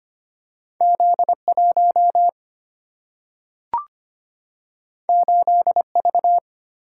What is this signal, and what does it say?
Morse code, heard as a keyed tone, about 25 wpm, 700 Hz. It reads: Z1 8V
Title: Single-Letter Single-Number 25wpm